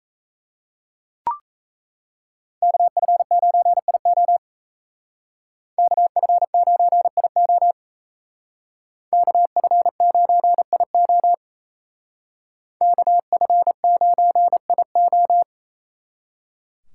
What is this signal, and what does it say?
Morse code, heard as a keyed tone, about 33 wpm, 700 Hz. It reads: KF9IO KF9IO KF9IO KF9IO
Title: Speed Racing - USA 2x2 Call Signs 28wpm